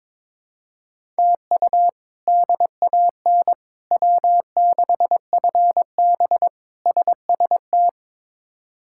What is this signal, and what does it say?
Morse code, heard as a keyed tone, about 22 wpm, 700 Hz. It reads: TU DAN W6FB SST
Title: K1USN SST QSO Practice - 22wpm